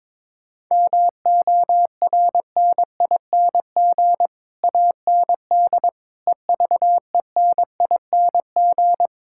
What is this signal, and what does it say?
Morse code, heard as a keyed tone, about 21 wpm, 700 Hz. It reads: MORNING AND EVENING
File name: Sets of 9 Words - 22wpm